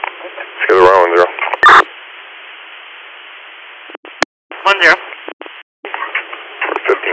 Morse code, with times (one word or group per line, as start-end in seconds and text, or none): none